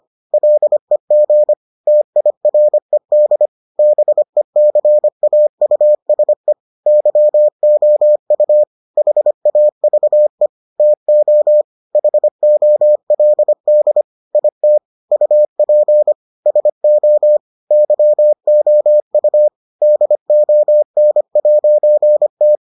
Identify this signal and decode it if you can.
Morse code, heard as a keyed tone, about 25 wpm, 600 Hz. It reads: LEG TIRED BECAUSE YOU HAVE TO HOLD IT UP SO YOU DON'T